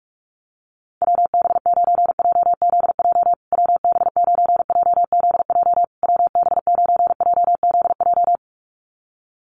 Morse code, heard as a keyed tone, about 44 wpm, 690 Hz. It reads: WB9JZJ WB9JZJ WB9JZJ
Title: USA Call Signs Repeated 3x 45wpm